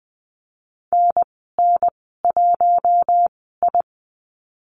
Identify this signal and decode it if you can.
Morse code, heard as a keyed tone, about 20 wpm, spaced as for 15 wpm, 700 Hz. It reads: NN1I
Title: 048 Lesson 9g - Number 4 - Call Signs 20wpm